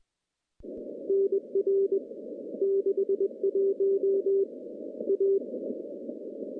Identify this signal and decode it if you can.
Morse code, heard as a keyed tone, about 23 wpm, 390 Hz. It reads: NR 61 A
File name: contests / CW-SS-k5tr- -0200